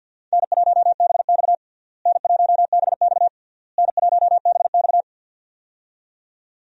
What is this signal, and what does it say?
Morse code, heard as a keyed tone, about 48 wpm, 700 Hz. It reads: N1BX N1BX N1BX
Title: USA Call Signs - 4 Characters - Repeated 3x - 50wpm